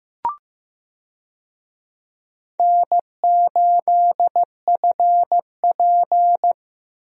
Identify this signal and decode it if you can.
Morse code, heard as a keyed tone, about 15 wpm, 700 Hz. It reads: N8FP
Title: USA Call Signs - 4 Characters - Repeated 3x - 15wpm